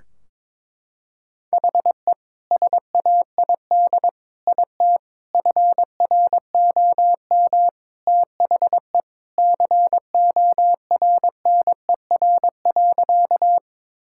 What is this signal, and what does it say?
Morse code, heard as a keyed tone, about 22 wpm, 700 Hz. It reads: HE SAID IT FROM THE CORNER.
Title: Sentences from Top 1600 Words 22wpm